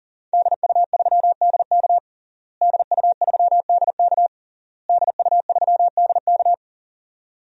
Morse code, heard as a keyed tone, about 40 wpm, 700 Hz. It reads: DU3DK DU3DK DU3DK
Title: DXCC-100 Call Signs - 5 Characters - Repeated 3x - 40wpm